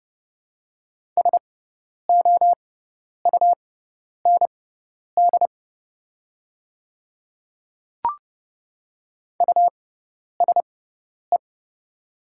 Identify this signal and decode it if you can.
Morse code, heard as a keyed tone, about 29 wpm, spaced as for 10 wpm, 690 Hz. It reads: SOUND USE
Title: 6x Character Spacing - Top 100 Words - 30wpm